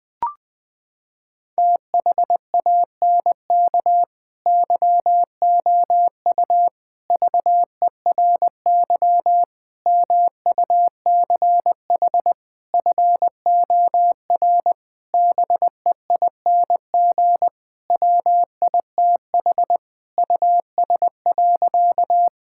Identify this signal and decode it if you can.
Morse code, heard as a keyed tone, about 20 wpm, 700 Hz. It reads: THANK YOU VERY MUCH FOR BEING WITH US.
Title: Sentences with 5-Word Phrases - 20wpm